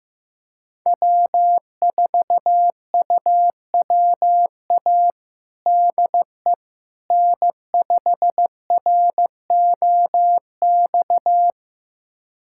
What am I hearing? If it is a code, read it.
Morse code, heard as a keyed tone, about 14 wpm, 700 Hz. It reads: W4UWA DE N5ROX
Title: USA Callsign DE Callsign - 5 Characters - 15wpm